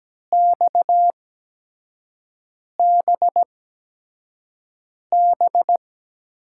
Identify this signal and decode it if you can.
Morse code, heard as a keyed tone, about 17 wpm, spaced as for 5 wpm, 700 Hz. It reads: XBB
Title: USA Call Signs - 5 Characters - 8x Character Spacing - 17wpm